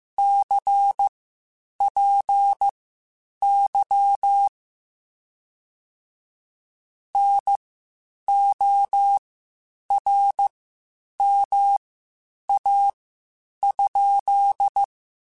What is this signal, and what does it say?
Morse code, heard as a keyed tone, about 15 wpm, spaced as for 8 wpm, 800 Hz. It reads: CPY NORMA?
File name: QSOs / QSOs-mp3s / WD0FBT 1@13